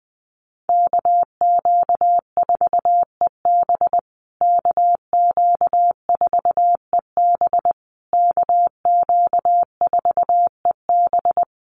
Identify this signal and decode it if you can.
Morse code, heard as a keyed tone, about 19 wpm, 700 Hz. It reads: KQ4EB KQ4EB KQ4EB
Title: USA Call Signs Repeated 3x 20wpm